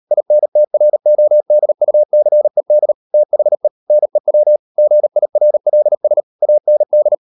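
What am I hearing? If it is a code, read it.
Morse code, heard as a keyed tone, about 38 wpm, 600 Hz. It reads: INTRODUCED THE NEW GIRLS AND
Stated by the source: Morse code practice at 38 WPM, sending Ken Follett's novel Never in English, from CW med Gnister.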